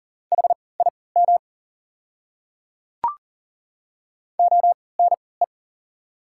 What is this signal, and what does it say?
Morse code, heard as a keyed tone, about 40 wpm, spaced as for 23 wpm, 690 Hz. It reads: HIM ONE
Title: 3x Character Spacing - Top 300 Words - 40wpm